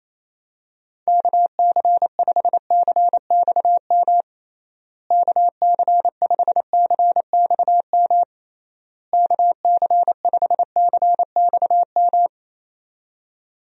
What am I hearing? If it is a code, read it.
Morse code, heard as a keyed tone, about 28 wpm, 700 Hz. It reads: KC5CXM KC5CXM KC5CXM
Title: USA Call Signs - 6 Characters - Repeated 3x - 28wpm